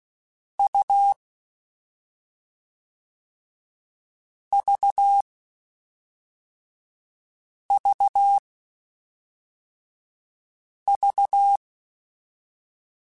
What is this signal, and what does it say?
Morse code, heard as a keyed tone, about 16 wpm, 800 Hz. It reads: U V V V